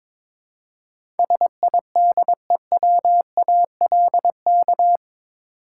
Morse code, heard as a keyed tone, about 21 wpm, 700 Hz. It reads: SIDEWALK
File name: practice / 22wpm / Eight Letter Words - 22wpm